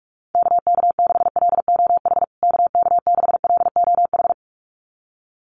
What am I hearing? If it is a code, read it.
Morse code, heard as a keyed tone, about 44 wpm, 700 Hz. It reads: KK6LOH KK6LOH
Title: USA Call Signs Repeated 2x 45wpm